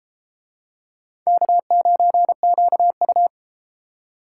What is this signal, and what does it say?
Morse code, heard as a keyed tone, about 32 wpm, 700 Hz. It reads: K9QU